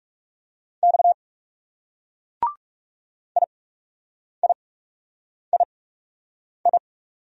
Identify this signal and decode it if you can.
Morse code, heard as a keyed tone, about 39 wpm, spaced as for 8 wpm, 700 Hz. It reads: X IIII